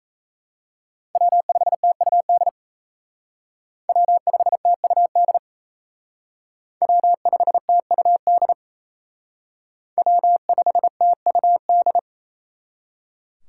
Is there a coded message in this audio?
Yes: Morse code from 1.1 to 12.0 s, heard as a keyed tone, about 36 wpm, 700 Hz. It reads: W5TUD W5TUD W5TUD W5TUD